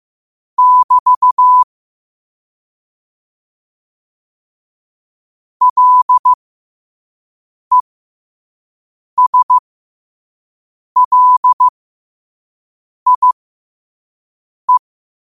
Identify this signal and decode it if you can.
Morse code, heard as a keyed tone, about 15 wpm, spaced as for 5 wpm, 1000 Hz. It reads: <BT> LESLIE